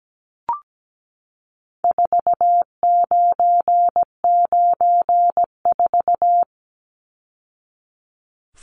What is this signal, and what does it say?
Morse code, heard as a keyed tone, about 17 wpm, 700 Hz. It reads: 4994
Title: Numbers - 4-Digit 17wpm